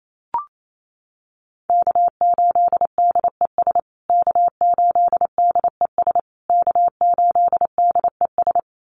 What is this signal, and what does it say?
Morse code, heard as a keyed tone, about 28 wpm, 700 Hz. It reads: K8DES K8DES K8DES
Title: USA Call Signs Repeated 3x 28wpm